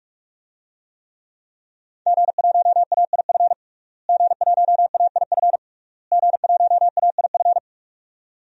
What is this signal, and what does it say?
Morse code, heard as a keyed tone, about 46 wpm, 700 Hz. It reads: G1AIF G1AIF G1AIF
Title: DXCC-100 Call Signs - 5 Characters - Repeated 3x - 45wpm